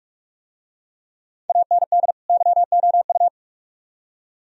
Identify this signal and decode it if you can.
Morse code, heard as a keyed tone, about 43 wpm, 700 Hz. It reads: AND YOU